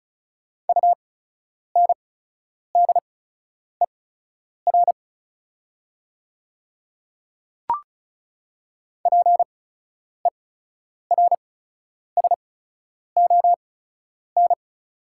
Morse code, heard as a keyed tone, about 34 wpm, spaced as for 10 wpm, 690 Hz. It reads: UNDER PERSON
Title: 8x Character Spacing - Top 500 Words - 35wpm